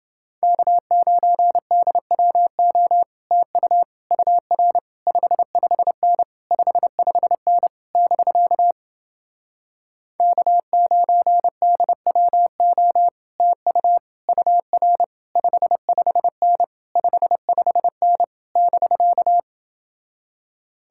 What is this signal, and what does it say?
Morse code, heard as a keyed tone, about 28 wpm, 700 Hz. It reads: K9DWO TU UR 55N 55N <BK> K9DWO TU UR 55N 55N <BK>